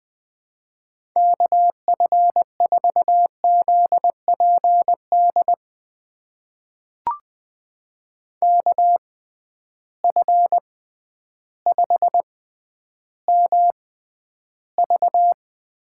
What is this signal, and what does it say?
Morse code, heard as a keyed tone, about 20 wpm, spaced as for 11 wpm, 700 Hz. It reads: KF4ZPD KF5MV
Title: USA Call Signs - 6 Characters - 6x Character Spacing - 20wpm